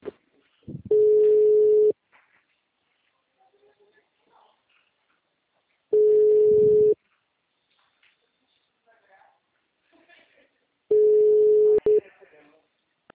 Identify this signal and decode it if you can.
Ringback tone (Europe, 425 Hz), with speech.
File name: out- -2010- - - .93732